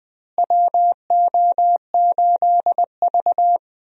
Morse code, heard as a keyed tone, about 20 wpm, 700 Hz. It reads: WO8V
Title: USA Call Signs - 4 Characters - Repeated 2x - 20wpm